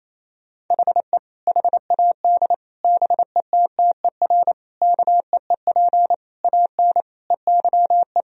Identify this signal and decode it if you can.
Morse code, heard as a keyed tone, about 28 wpm, 700 Hz. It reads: HE HAD BETTER KEEP AN EYE
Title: Sentences from Top 1000 Words - Review Part 1 - 28wpm